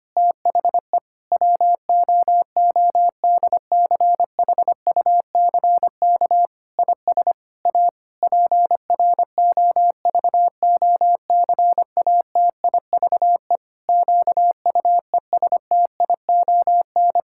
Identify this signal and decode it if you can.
Morse code, heard as a keyed tone, about 25 wpm, 700 Hz. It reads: THE WOODCHUCK IS A PROVOCATIVE QUESTION
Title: Pangram Challenge v2 - 25wpm